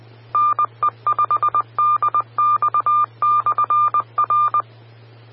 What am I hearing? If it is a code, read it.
Morse code, heard as a keyed tone, about 20 wpm, 1200 Hz. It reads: NE5DX/R
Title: Austin, TX Amateur Repeaters